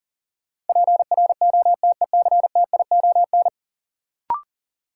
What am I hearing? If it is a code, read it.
Morse code, heard as a keyed tone, about 40 wpm, 700 Hz. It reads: PROTECTION
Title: Sentences from Top 1600 Words - New Word Drill Alt - 40wpm